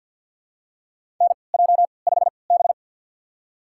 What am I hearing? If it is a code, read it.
Morse code, heard as a keyed tone, about 53 wpm, spaced as for 29 wpm, 700 Hz. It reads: NJ5B
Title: USA Call Signs - 4 Characters - 3x Character Spacing - 50wpm